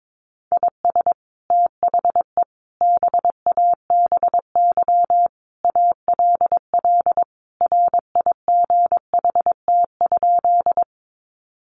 Morse code, heard as a keyed tone, about 22 wpm, 700 Hz. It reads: IS THE BABY ALL RIGHT?
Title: Sentences from Top 1300 Words 22wpm